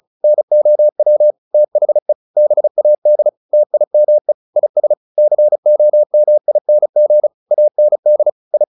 Morse code, heard as a keyed tone, about 34 wpm, 600 Hz. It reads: NOW THE BAD TIME IS COMING AND I
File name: the old man and the sea 2 35wpm